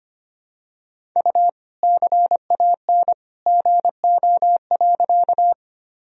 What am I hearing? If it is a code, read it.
Morse code, heard as a keyed tone, about 25 wpm, 700 Hz. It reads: U CAN GO.
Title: Sentences from Top 100 Words and Abbreviations 25wpm